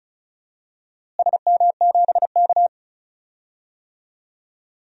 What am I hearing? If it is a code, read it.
Morse code, heard as a keyed tone, about 36 wpm, 700 Hz. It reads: SM7K
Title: DXCC-100 Call Signs - 4 Characters - Repeated 2x - 35wpm